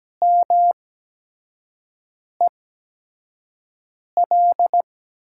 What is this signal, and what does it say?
Morse code, heard as a keyed tone, about 17 wpm, spaced as for 5 wpm, 700 Hz. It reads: MEL
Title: USA Call Signs - 6 Characters - 8x Character Spacing - 17wpm